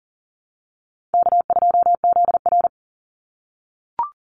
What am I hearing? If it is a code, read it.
Morse code, heard as a keyed tone, about 42 wpm, 700 Hz. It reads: K2ZR
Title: Easy Call Signs - 40wpm